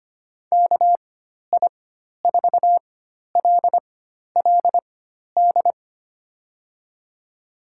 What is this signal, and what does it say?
Morse code, heard as a keyed tone, about 25 wpm, spaced as for 12 wpm, 700 Hz. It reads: KI4LLD
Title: USA Call Signs - 4x Character Spacing - 25wpm